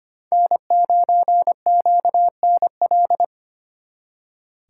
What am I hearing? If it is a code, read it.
Morse code, heard as a keyed tone, about 32 wpm, 700 Hz. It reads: N9QNL